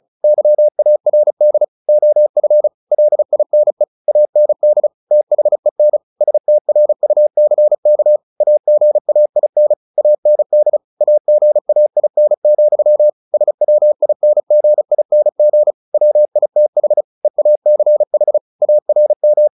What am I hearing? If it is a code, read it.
Morse code, heard as a keyed tone, about 34 wpm, 600 Hz. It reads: YARD OF LINE AND THEN STRUCK AGAIN AND AGAIN, SWINGING WITH EACH ARM
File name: the old man and the sea / the old man and the sea 1 35wpm